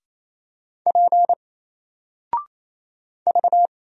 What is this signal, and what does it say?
Morse code, heard as a keyed tone, about 27 wpm, 700 Hz. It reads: P V
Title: Single Letter-Number 28wpm